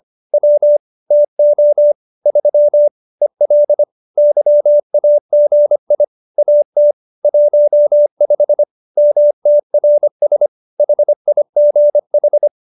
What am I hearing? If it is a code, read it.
Morse code, heard as a keyed tone, about 25 wpm, 590 Hz. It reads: W TO 3 EL YAGI AT 15 MTRS HIGH